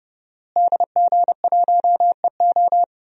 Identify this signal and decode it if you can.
Morse code, heard as a keyed tone, about 31 wpm, 700 Hz. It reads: DG1EO
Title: DXCC-100 Call Signs - 5 Characters - Repeated 3x - 30wpm